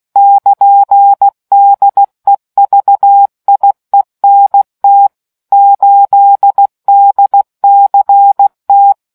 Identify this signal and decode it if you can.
Morse code, heard as a keyed tone, about 16 wpm, 800 Hz. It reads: <KN>DEVIENT 8DCT